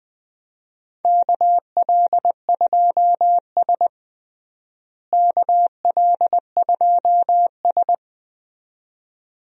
Morse code, heard as a keyed tone, about 21 wpm, 700 Hz. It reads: KL2S KL2S
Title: USA Call Signs - 4 Characters - Repeated 2x - 20wpm